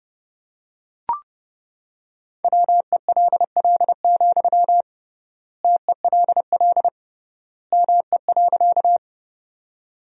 Morse code, heard as a keyed tone, about 30 wpm, 700 Hz. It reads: WELL, TELL ME.
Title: 3x Word-Spacing - Sentences from Top 200 Words - 30wpm